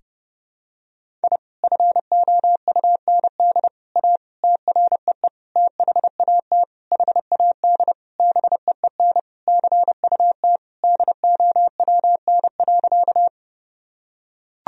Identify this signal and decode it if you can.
Morse code, heard as a keyed tone, about 30 wpm, 700 Hz. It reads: I FOUND A TREE THAT HAD BEEN CUT DOWN.
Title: Sentences from Top 1000 Words - Review Part 1 - 30wpm